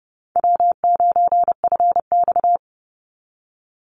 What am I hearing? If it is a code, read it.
Morse code, heard as a keyed tone, about 30 wpm, 700 Hz. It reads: W9FX
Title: Easy Call Signs - 30wpm